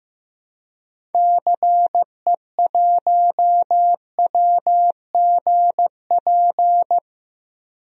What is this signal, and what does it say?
Morse code, heard as a keyed tone, about 15 wpm, 700 Hz. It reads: CE1WGP
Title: DXCC-100 Call Signs - 6 Characters - Repeated 3x - 15wpm